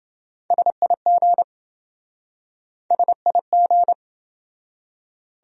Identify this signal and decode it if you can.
Morse code, heard as a keyed tone, about 27 wpm, 700 Hz. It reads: SIG SIG